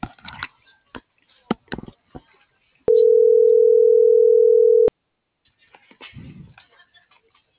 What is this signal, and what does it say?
Ringback tone (North America), 440 + 480 Hz.
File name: CENTRAL GOMSA / out cisneros.hector